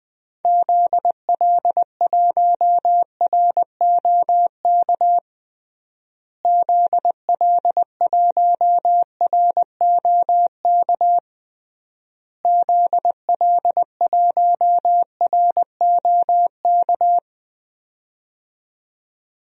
Morse code, heard as a keyed tone, about 20 wpm, 700 Hz. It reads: ZL1ROK ZL1ROK ZL1ROK
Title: DXCC-100 Call Signs - 6 Characters - Repeated 3x - 20wpm